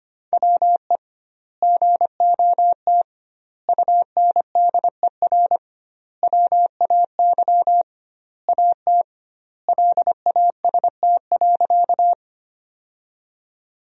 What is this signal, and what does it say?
Morse code, heard as a keyed tone, about 25 wpm, 700 Hz. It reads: WE GOT UNDER WAY AT LAST.
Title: 2x Word-Spacing - Sentences from Top 300 Words - 25wpm